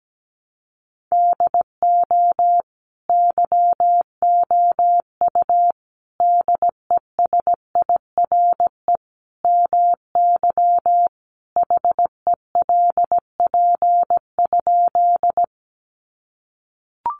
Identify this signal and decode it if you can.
Morse code, heard as a keyed tone, about 17 wpm, 700 Hz. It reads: DO YOU DESIRE MY HELP?
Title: Sentences from Top 1100 Words 17wpm